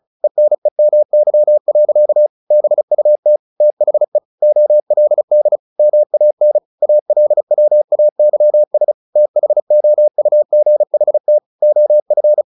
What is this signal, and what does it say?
Morse code, heard as a keyed tone, about 35 wpm, 600 Hz. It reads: ENEMY. BUT THE OLD MAN ALWAYS THOUGHT OF
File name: the old man and the sea / the old man and the sea 1 35wpm